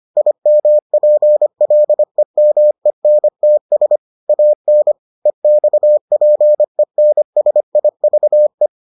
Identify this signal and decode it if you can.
Morse code, heard as a keyed tone, about 25 wpm, 600 Hz. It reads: IMPLEMENTS AN EXPENSIVE